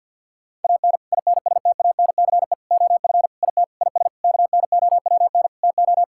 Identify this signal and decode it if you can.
Morse code, heard as a keyed tone, about 48 wpm, 690 Hz. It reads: AN INSTANCE OF IT IS KNOWN TO
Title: Sentences from Top 1000 Words - Review Part 5 - 50wpm